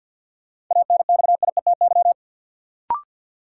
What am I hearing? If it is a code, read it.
Morse code, heard as a keyed tone, about 52 wpm, 700 Hz. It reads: ANXIETY
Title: Sentences from Top 1600 Words - New Word Drill Alt - 50wpm